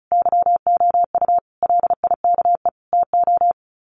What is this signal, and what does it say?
Morse code, heard as a keyed tone, about 34 wpm, 700 Hz. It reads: YOU LIKE TO
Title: Sets of 4 Words 35wpm